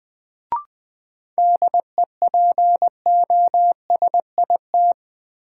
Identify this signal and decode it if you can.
Morse code, heard as a keyed tone, about 20 wpm, 700 Hz. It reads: DEPOSIT
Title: Sentences from Top 1300 Words - New Word Drill - 20wpm